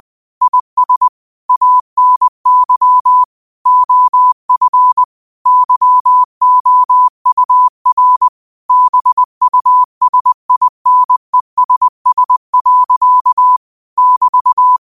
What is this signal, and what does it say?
Morse code, heard as a keyed tone, about 20 wpm, 1000 Hz. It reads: IS ANY OF YOUR BUSINESS. <BT>
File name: qotdfiles / 20wpm